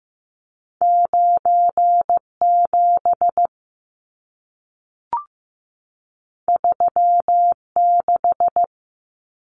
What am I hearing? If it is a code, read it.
Morse code, heard as a keyed tone, about 15 wpm, 700 Hz. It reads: 97 36
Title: Numbers - 2-Digit 15wpm